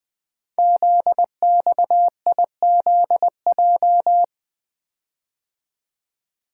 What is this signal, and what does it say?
Morse code, heard as a keyed tone, about 20 wpm, 700 Hz. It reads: ZXIZJ
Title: Code Groups - 5 Letters - 20wpm